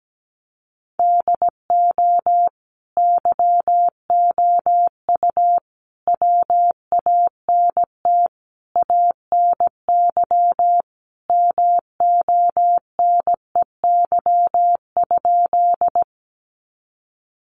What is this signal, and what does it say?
Morse code, heard as a keyed tone, about 17 wpm, 700 Hz. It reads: DO YOU WANT ANY MONEY?
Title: Sentences from Top 400 Words 17wpm